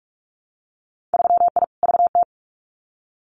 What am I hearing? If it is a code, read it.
Morse code, heard as a keyed tone, about 44 wpm, 700 Hz. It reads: 3I VT
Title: Winter Field Day Exchanges Encore 45wpm